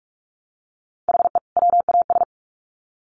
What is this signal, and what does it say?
Morse code, heard as a keyed tone, about 44 wpm, 700 Hz. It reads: HE WAS
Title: Sets of 2 Words 45wpm